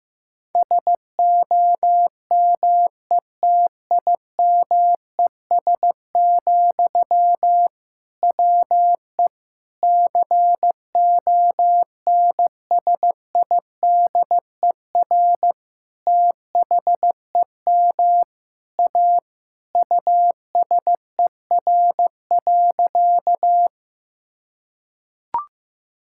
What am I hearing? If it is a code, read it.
Morse code, heard as a keyed tone, about 15 wpm, 700 Hz. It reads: SOMETIMES, WE CONSIDER THEM A USER.
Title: Sentences from Top 1000 Words - Review Part 5 - 15wpm